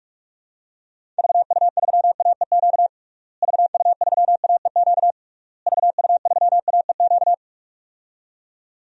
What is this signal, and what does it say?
Morse code, heard as a keyed tone, about 41 wpm, 700 Hz. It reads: VU3AEQ VU3AEQ VU3AEQ
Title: DXCC-100 Call Signs - 6 Characters - Repeated 3x - 45wpm